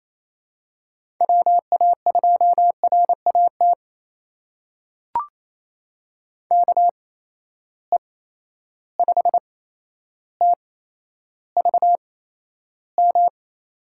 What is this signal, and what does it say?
Morse code, heard as a keyed tone, about 28 wpm, spaced as for 12 wpm, 700 Hz. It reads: WA2RAT KE5TVM
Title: USA Call Signs - 6 Characters - 8x Character Spacing - 28wpm